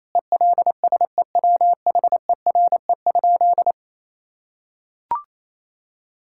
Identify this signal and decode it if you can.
Morse code, heard as a keyed tone, about 28 wpm, 700 Hz. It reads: ELSEWHERE?
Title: Sentences from Top 1600 Words 28wpm